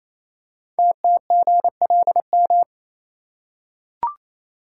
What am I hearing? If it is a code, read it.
Morse code, heard as a keyed tone, about 28 wpm, 700 Hz. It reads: TTGLM
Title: Code Groups - 5 Letters - 28wpm